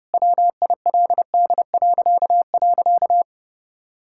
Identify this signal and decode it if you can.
Morse code, heard as a keyed tone, about 30 wpm, 700 Hz. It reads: WILD..
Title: Pangram Challenge v2 - 30wpm